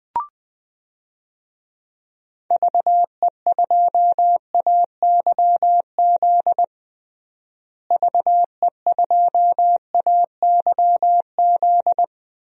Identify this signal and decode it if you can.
Morse code, heard as a keyed tone, about 20 wpm, 700 Hz. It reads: VE2AYZ VE2AYZ
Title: DXCC-100 Call Signs - 6 Characters - Repeated 3x - 20wpm